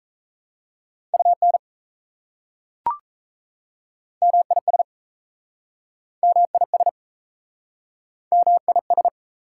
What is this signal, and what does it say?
Morse code, heard as a keyed tone, about 36 wpm, 700 Hz. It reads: UN MIS MIS MIS